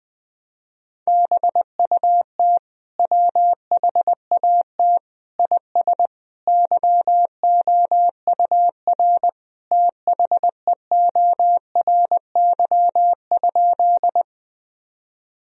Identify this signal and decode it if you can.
Morse code, heard as a keyed tone, about 21 wpm, 700 Hz. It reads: BUT WHAT IS YOUR THEORY?
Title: Sentences from Top 1000 Words - Review Part 3 - 20wpm